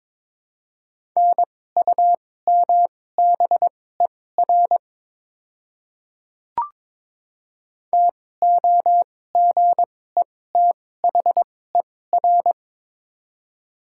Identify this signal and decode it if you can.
Morse code, heard as a keyed tone, about 22 wpm, spaced as for 16 wpm, 700 Hz. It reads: NUMBER TOGETHER
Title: 2x Character Spacing - Top 300 Words - 22wpm